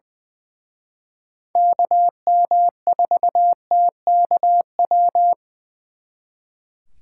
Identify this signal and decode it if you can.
Morse code, heard as a keyed tone, about 24 wpm, 700 Hz. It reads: KM4TKW
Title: Speed Racing - USA 2x3 Call Signs 20wpm